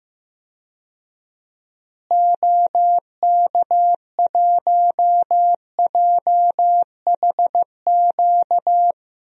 Morse code, heard as a keyed tone, about 16 wpm, 700 Hz. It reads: OK1JHQ
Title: DXCC-100 Call Signs - 6 Characters - Repeated 2x - 15wpm